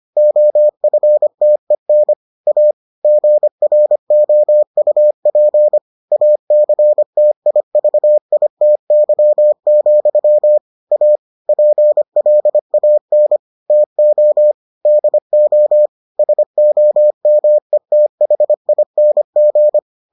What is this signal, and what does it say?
Morse code, heard as a keyed tone, about 25 wpm, 600 Hz. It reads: OFTEN A GROUP ACTIVITY, A PLAN TO DO SOMETHING